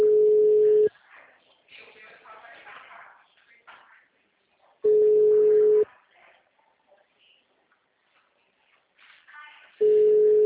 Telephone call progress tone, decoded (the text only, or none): ringback tone